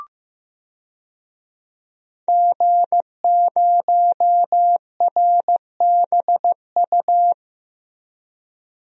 Morse code, heard as a keyed tone, about 15 wpm, 700 Hz. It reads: G0RBU
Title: DXCC-100 Call Signs - 5 Characters - Repeated 3x - 15wpm